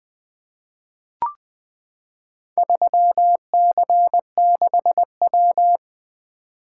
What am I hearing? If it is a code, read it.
Morse code, heard as a keyed tone, about 20 wpm, 700 Hz. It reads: 3C6W
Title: Four Letter-Number Combinations - 20wpm